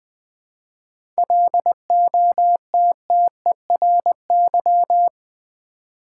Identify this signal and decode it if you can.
Morse code, heard as a keyed tone, about 20 wpm, 700 Hz. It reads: LOTTERY
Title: 7 Letter Words - First 3 Letters Given - 20wpm